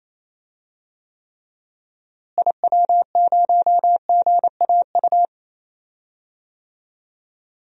Morse code, heard as a keyed tone, about 28 wpm, 700 Hz. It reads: IW0GAU
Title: DXCC-100 Call Signs - 6 Characters - Repeated 3x - 28wpm